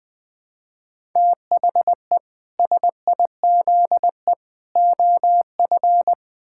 Morse code, heard as a keyed tone, about 21 wpm, 700 Hz. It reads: THE SIZE OF
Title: Build A Phrase - 3-Word Phrases - 20wpm